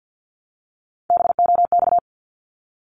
Morse code, heard as a keyed tone, about 48 wpm, 700 Hz. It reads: BOX
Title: Top 500 Words Repeated 50wpm